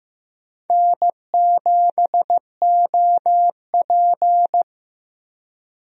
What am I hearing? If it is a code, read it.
Morse code, heard as a keyed tone, about 15 wpm, 700 Hz. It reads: N7OP
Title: USA Call Signs - 4 Characters - 15wpm